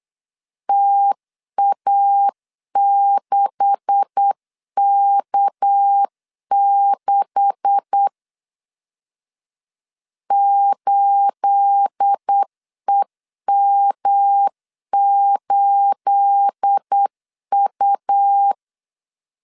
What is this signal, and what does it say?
Morse code, heard as a keyed tone, about 8 wpm, 800 Hz. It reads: TA6K6 8EM8U
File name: LEZIONI / 9 286 8wpm 3425